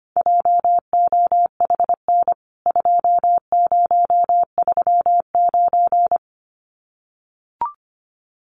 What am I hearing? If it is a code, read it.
Morse code, heard as a keyed tone, about 25 wpm, 700 Hz. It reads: JOHN 2039
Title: CWT Exchange 25wpm